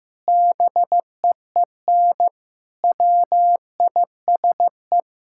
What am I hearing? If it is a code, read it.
Morse code, heard as a keyed tone, about 15 wpm, 700 Hz. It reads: BEEN WISE
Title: Sentences from Top 1600 Words 15wpm